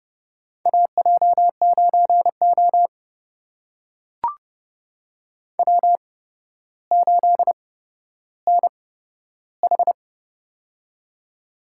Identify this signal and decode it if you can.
Morse code, heard as a keyed tone, about 31 wpm, spaced as for 13 wpm, 700 Hz. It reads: AJ9O W8NH
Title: USA Call Signs - 4 Characters - 8x Character Spacing - 30wpm